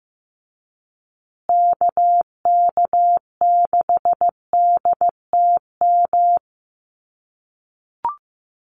Morse code, heard as a keyed tone, about 14 wpm, 700 Hz. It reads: KK6DTM
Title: USA Call Signs Repeated 3x 15wpm